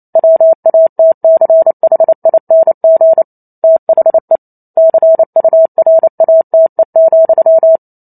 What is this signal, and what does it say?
Morse code, heard as a keyed tone, about 28 wpm, 650 Hz. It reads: WATCHING THE CURATE,